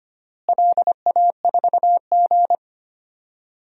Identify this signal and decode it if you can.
Morse code, heard as a keyed tone, about 25 wpm, 700 Hz. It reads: LA4G
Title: DXCC-100 Call Signs - 4 Characters - Repeated 2x - 25wpm